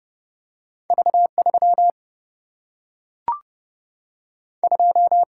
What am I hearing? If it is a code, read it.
Morse code, heard as a keyed tone, about 30 wpm, 700 Hz. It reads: V 3 2
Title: Single-Letter Single-Number 30wpm